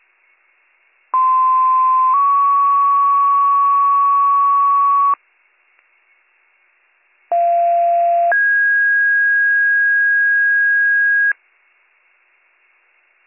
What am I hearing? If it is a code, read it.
Two-tone page, heard as a one-second tone, then a three-second tone, about 1040 then 1120 Hz, about 690 then 1650 Hz. It is heard from 1.1 to 11.3 s.